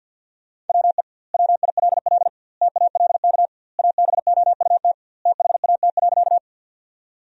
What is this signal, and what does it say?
Morse code, heard as a keyed tone, about 50 wpm, 700 Hz. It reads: WE WILL TALK ABOUT THAT.
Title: 2x Word-Spacing - Sentences from Top 400 Words - 50wpm